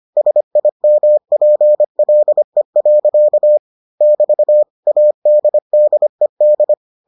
Morse code, heard as a keyed tone, about 25 wpm, 600 Hz. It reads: SIMPLE. <BT> ADDED